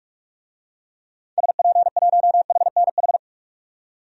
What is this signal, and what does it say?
Morse code, heard as a keyed tone, about 46 wpm, 700 Hz. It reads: SP1HNH
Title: DXCC-100 Call Signs - 6 Characters - 45wpm